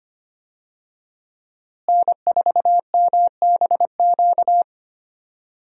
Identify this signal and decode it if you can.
Morse code, heard as a keyed tone, about 25 wpm, 700 Hz. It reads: N4MBQ
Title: USA Call Signs - 5 Characters - Repeated 3x - 25wpm